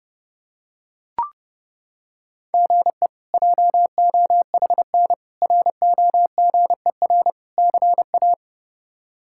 Morse code, heard as a keyed tone, about 30 wpm, 700 Hz. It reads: GE JOHN ROGER CA
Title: K1USN SST QSO Practice - 30wpm